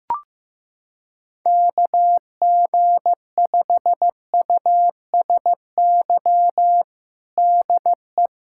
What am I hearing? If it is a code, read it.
Morse code, heard as a keyed tone, about 15 wpm, 700 Hz. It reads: KG5USY DE
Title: USA Callsign DE Callsign - 6 Characters - 15wpm